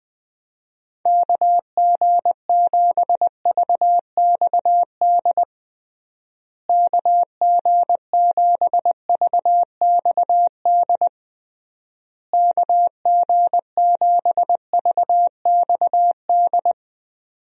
Morse code, heard as a keyed tone, about 20 wpm, 700 Hz. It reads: KG7VXD KG7VXD KG7VXD
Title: USA Call Signs - 6 Characters - Repeated 3x - 20wpm